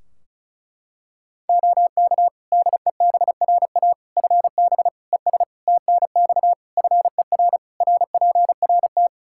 Morse code, heard as a keyed tone, about 34 wpm, 700 Hz. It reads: OK DEBRA FB ES TNX FER RPRT
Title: Practice QSO - Exchange 2 - Info - 35wpm